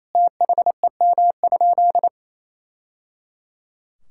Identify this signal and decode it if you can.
Morse code, heard as a keyed tone, about 28 wpm, 700 Hz. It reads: THEM?
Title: Sentences from Top 1000 Words - Review Part 5 - 28wpm